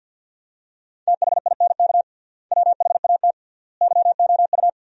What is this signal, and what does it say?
Morse code, heard as a keyed tone, about 50 wpm, 700 Hz. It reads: THINK WHAT YOU
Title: 3x Word-Spacing - Sentences from Top 200 Words - 50wpm